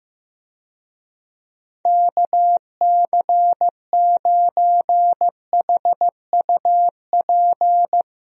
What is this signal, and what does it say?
Morse code, heard as a keyed tone, about 15 wpm, 700 Hz. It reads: KC9HUP
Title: USA Call Signs - 6 Characters - Repeated 3x - 15wpm